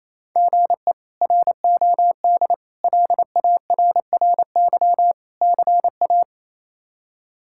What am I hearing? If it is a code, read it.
Morse code, heard as a keyed tone, about 27 wpm, 700 Hz. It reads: GE ROD LARRY CA
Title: K1USN SST QSO Practice - 28wpm